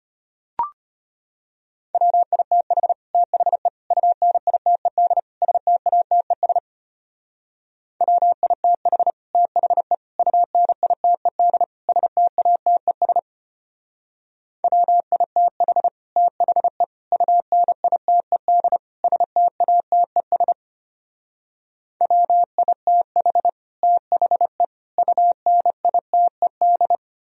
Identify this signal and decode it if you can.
Morse code, heard as a keyed tone, about 32 wpm, 700 Hz. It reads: WITH THE UNITED STATES WITH THE UNITED STATES WITH THE UNITED STATES WITH THE UNITED